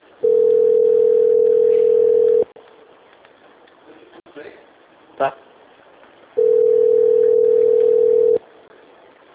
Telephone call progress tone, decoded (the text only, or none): ringback tone